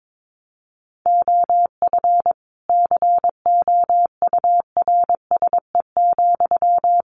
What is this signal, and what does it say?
Morse code, heard as a keyed tone, about 22 wpm, 700 Hz. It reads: OF COURSE,
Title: Sentences from Top 1500 Words 22wpm